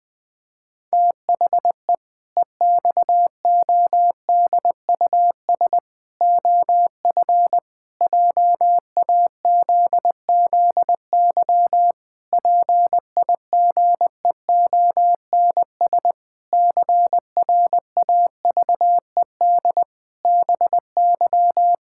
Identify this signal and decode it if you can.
Morse code, heard as a keyed tone, about 20 wpm, 700 Hz. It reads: THE EXODUS OF JAZZY PIGEONS CRAVED BY
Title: Pangram Challenge v2 - 20wpm